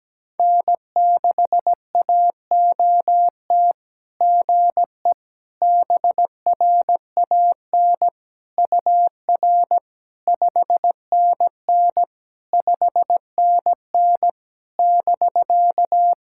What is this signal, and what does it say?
Morse code, heard as a keyed tone, about 17 wpm, 700 Hz. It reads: N6AOT GE BRAN UR 5NN 5NN <BK>
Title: SOTA QSO Practice 17wpm